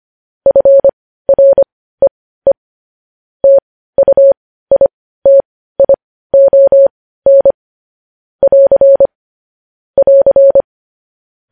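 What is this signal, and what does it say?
Morse code, heard as a keyed tone, about 25 wpm, spaced as for 15 wpm, 550 Hz. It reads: FREE TUITION <AR> <AR>
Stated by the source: Morse news headlines practice, possibly sent at 15 wpm.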